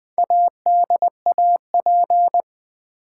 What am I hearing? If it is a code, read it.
Morse code, heard as a keyed tone, about 20 wpm, 700 Hz. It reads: ADAP
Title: Build A Word - 4 Letters At A Time - 10-Letter Words - 20wpm